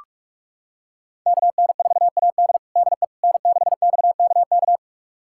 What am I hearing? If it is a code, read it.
Morse code, heard as a keyed tone, about 41 wpm, 700 Hz. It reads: KN4AD DE N6XKK
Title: USA Callsign DE Callsign - 5 Characters - 45wpm